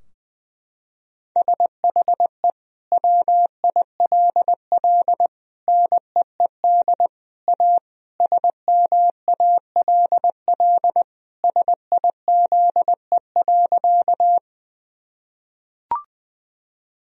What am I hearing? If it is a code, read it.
Morse code, heard as a keyed tone, about 21 wpm, 700 Hz. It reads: SHE WILL NEED A SMALL SIZE.
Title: Sentences from Top 1000 Words - Review Part 2 - 20wpm